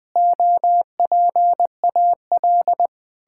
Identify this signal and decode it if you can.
Morse code, heard as a keyed tone, about 20 wpm, 700 Hz. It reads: OPAL
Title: Pangram Challenge v2 - 20wpm